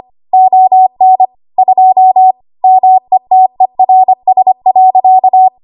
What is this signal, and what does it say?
Morse code, heard as a keyed tone, about 25 wpm, 750 Hz. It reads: ON 2 METERS.
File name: Morse / 25WPM